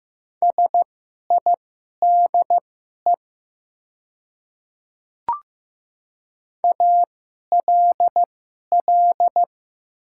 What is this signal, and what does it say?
Morse code, heard as a keyed tone, about 15 wpm, spaced as for 11 wpm, 690 Hz. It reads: SIDE ALL
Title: 2x Character Spacing - Top 100 Words - 15wpm